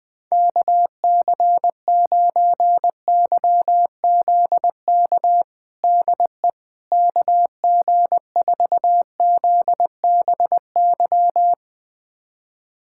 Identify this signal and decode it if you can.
Morse code, heard as a keyed tone, about 20 wpm, 700 Hz. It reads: KC9YZK DE KG4ZBY
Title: USA Callsign DE Callsign 20wpm